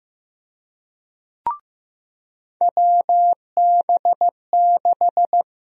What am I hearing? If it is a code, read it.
Morse code, heard as a keyed tone, about 15 wpm, 700 Hz. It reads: WB6
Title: USA Call Sign Prefixes - 15wpm